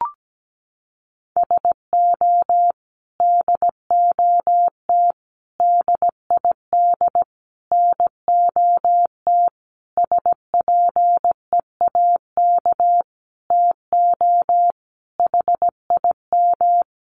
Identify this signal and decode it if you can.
Morse code, heard as a keyed tone, about 17 wpm, 700 Hz. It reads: SO DOT DID NOT SPEAK TO HIM
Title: Sentences from Top 1500 Words 17wpm